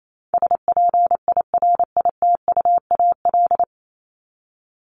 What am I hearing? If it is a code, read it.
Morse code, heard as a keyed tone, about 28 wpm, 700 Hz. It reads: SPIRITUAL
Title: Top 1501-1750 Words 28wpm